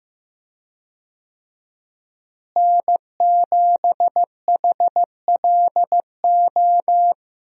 Morse code, heard as a keyed tone, about 15 wpm, 700 Hz. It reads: N7HLO
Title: USA Call Signs - 5 Characters - Repeated 2x - 15wpm